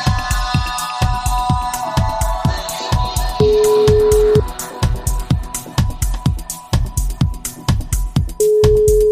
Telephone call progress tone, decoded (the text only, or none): ringback tone